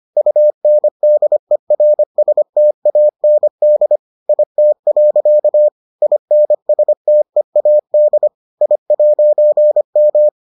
Morse code, heard as a keyed tone, about 25 wpm, 600 Hz. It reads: UNDERSTAND IT. INSTEAD I'M